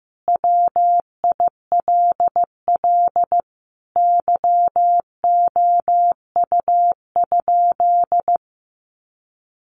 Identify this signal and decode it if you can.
Morse code, heard as a keyed tone, about 15 wpm, 700 Hz. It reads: WILL YOU?
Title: Sentences from Top 1100 Words 15wpm